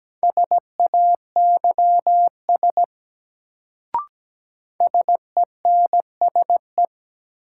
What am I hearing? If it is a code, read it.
Morse code, heard as a keyed tone, about 17 wpm, 700 Hz. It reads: SAYS SENSE
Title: Sets of 2 Words - New Word Drill - 17wpm